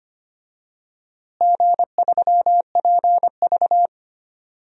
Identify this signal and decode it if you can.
Morse code, heard as a keyed tone, about 25 wpm, 700 Hz. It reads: G3PV
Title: DXCC-100 Call Signs - 4 Characters - Repeated 3x - 25wpm